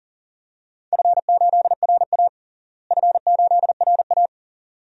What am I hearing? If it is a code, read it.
Morse code, heard as a keyed tone, about 40 wpm, 700 Hz. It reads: F8RA F8RA
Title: DXCC-100 Call Signs - 4 Characters - Repeated 2x - 40wpm